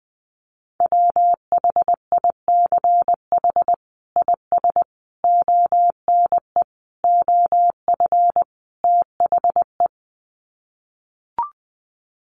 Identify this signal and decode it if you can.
Morse code, heard as a keyed tone, about 20 wpm, 700 Hz. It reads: WHICH IS ONE OF THE
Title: Sets of 5 Words Encore 20wpm